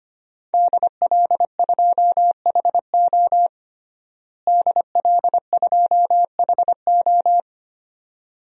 Morse code, heard as a keyed tone, about 25 wpm, 700 Hz. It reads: DL2HO DL2HO
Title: DXCC-100 Call Signs - 5 Characters - Repeated 2x - 25wpm